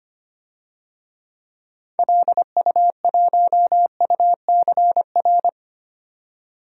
Morse code, heard as a keyed tone, about 25 wpm, 700 Hz. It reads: LU1UCR
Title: DXCC-100 Call Signs - 6 Characters - 25wpm